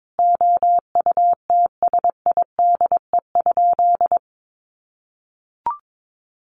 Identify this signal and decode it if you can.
Morse code, heard as a keyed tone, about 22 wpm, 700 Hz. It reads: OUTSIDE?
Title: Sentences from Top 1200 Words 22wpm